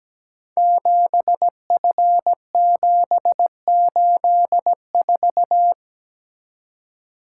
Morse code, heard as a keyed tone, about 17 wpm, 690 Hz. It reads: 7F784
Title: Code Groups - 5 Letter:Numbers - 17wpm